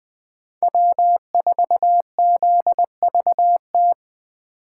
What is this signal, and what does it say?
Morse code, heard as a keyed tone, about 19 wpm, 700 Hz. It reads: W4ZVT